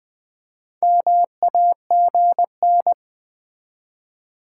Morse code, heard as a keyed tone, about 20 wpm, 700 Hz. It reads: MAGN
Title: Build A Word - 2 Letters At A Time - 8-Letter Words - 20wpm